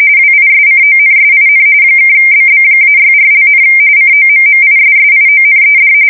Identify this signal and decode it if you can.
RTTY, about 45.45 baud. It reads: WISCONSIN (BB-64), COMMISSIONED
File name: fsk170 / battleships